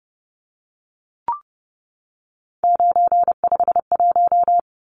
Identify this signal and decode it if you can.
Morse code, heard as a keyed tone, about 30 wpm, 700 Hz. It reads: 951
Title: Numbers - 3-Digit 30wpm